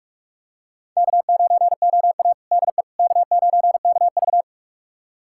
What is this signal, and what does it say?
Morse code, heard as a keyed tone, about 43 wpm, 700 Hz. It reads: K9OA DE K9KV
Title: USA Callsign DE Callsign - 4 Characters - 45wpm